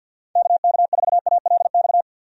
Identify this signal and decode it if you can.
Morse code, heard as a keyed tone, about 47 wpm, 700 Hz. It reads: KK4ALX
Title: USA Call Signs - 6 Characters - 50wpm